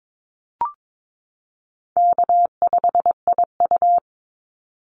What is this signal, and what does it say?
Morse code, heard as a keyed tone, about 22 wpm, 700 Hz. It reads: K5IU
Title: Call Signs - 22wpm